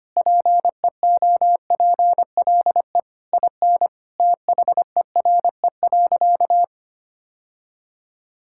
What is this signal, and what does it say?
Morse code, heard as a keyed tone, about 25 wpm, 700 Hz. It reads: PEOPLE IN THERE.
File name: practice / 25wpm / Sentences with 3-Word Phrases - 25wpm